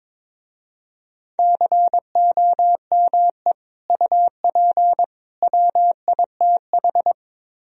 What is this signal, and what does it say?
Morse code, heard as a keyed tone, about 25 wpm, 700 Hz. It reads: COME UP WITH
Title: Speed-Racing - Sets of 3 Words 22wpm